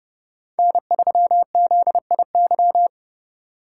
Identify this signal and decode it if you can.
Morse code, heard as a keyed tone, about 30 wpm, 700 Hz. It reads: N3ZIY